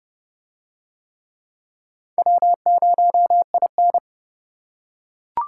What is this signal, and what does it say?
Morse code, heard as a keyed tone, about 28 wpm, 700 Hz. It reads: W0IN